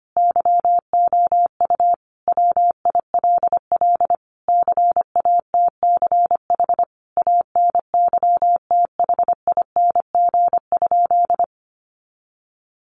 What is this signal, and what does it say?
Morse code, heard as a keyed tone, about 25 wpm, 700 Hz. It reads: YOU WILL CATCH ANYTHING?
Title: Sentences from Top 1500 Words 25wpm